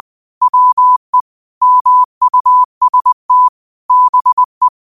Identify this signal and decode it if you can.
Morse code, heard as a keyed tone, about 20 wpm, 1000 Hz. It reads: WE MUST BE